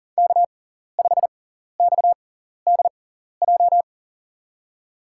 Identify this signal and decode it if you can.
Morse code, heard as a keyed tone, about 38 wpm, spaced as for 14 wpm, 700 Hz. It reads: K5XDJ
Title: USA Call Signs - 5 Characters - 6x Character Spacing - 40wpm